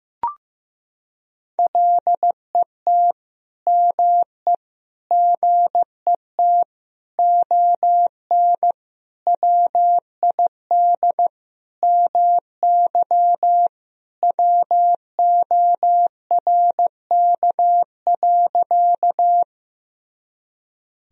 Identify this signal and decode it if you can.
Morse code, heard as a keyed tone, about 15 wpm, 700 Hz. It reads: LET ME GET ON WID MY WORK.
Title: Sentences from Top 300 Words and Abbreviations 15wpm